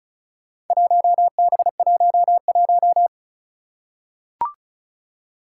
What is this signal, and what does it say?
Morse code, heard as a keyed tone, about 36 wpm, 700 Hz. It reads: 1B11
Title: Four Challenging Letter-Number Combinations 35wpm